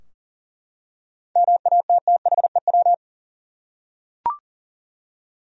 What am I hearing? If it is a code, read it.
Morse code, heard as a keyed tone, about 40 wpm, 700 Hz. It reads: MATTHEW
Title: USA Common Names - 40wpm